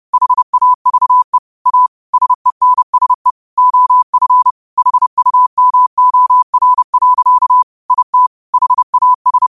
Morse code, heard as a keyed tone, about 30 wpm, 1000 Hz. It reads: HAVE A SENSE OF HUMOR. IT HAS
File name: qotdfiles / 30wpm